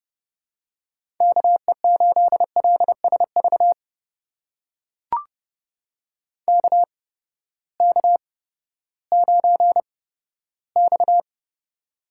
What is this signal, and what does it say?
Morse code, heard as a keyed tone, about 32 wpm, spaced as for 15 wpm, 700 Hz. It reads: KE8LSV KK9X
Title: USA Call Signs - 8x Character Spacing - 30wpm